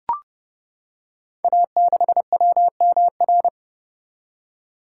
Morse code, heard as a keyed tone, about 30 wpm, 700 Hz. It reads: A6WMR
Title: Code Groups - 5 Letter:Numbers - 30wpm